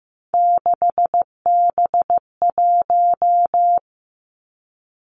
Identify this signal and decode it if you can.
Morse code, heard as a keyed tone, about 15 wpm, 700 Hz. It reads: 6B1
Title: Three Challenging Letter-Number Combinations 15wpm